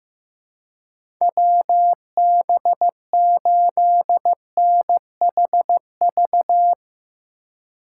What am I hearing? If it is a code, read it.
Morse code, heard as a keyed tone, about 15 wpm, 700 Hz. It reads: WB8NHV
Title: USA Call Signs - 6 Characters - 8x Character Spacing - 15wpm